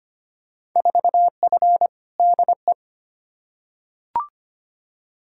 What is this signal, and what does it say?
Morse code, heard as a keyed tone, about 25 wpm, 700 Hz. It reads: 4F DE
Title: Field Day Exchanges Encore - 25wpm